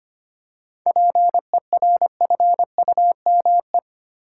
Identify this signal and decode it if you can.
Morse code, heard as a keyed tone, about 25 wpm, 690 Hz. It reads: PERFUME
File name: practice / 25wpm / Easy Words with Hints - 25wpm